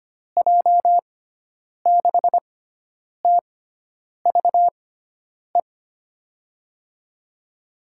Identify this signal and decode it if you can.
Morse code, heard as a keyed tone, about 25 wpm, spaced as for 9 wpm, 700 Hz. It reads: J6TVE
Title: USA Call Signs - 6x Character Spacing - 25wpm